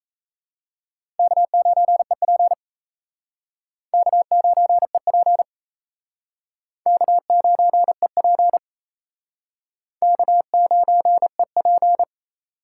Morse code, heard as a keyed tone, about 34 wpm, 700 Hz. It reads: K9EP K9EP K9EP K9EP
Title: Speed Racing - USA 1x2 Call Signs 28wpm